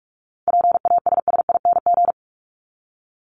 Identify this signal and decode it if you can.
Morse code, heard as a keyed tone, about 44 wpm, 700 Hz. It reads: PASSING
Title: ING Suffix - Top 100 Words with Base Words 45wpm